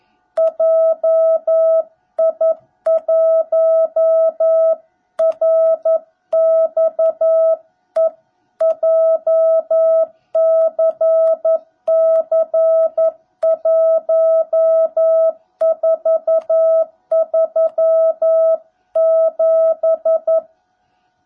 Morse code, heard as a keyed tone, about 11 wpm, 660 Hz. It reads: JI1RXE JCC1437
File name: jh8cbh / 2021cw / cw17b